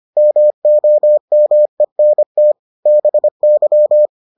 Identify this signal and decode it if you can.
Morse code, heard as a keyed tone, about 25 wpm, 600 Hz. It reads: MOMENT BY